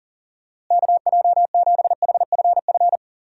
Morse code, heard as a keyed tone, about 40 wpm, 700 Hz. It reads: KJ7HFF
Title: USA Call Signs - 6 Characters - Repeated 3x - 40wpm